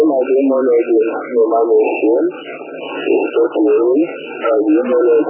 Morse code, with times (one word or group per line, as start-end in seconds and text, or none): none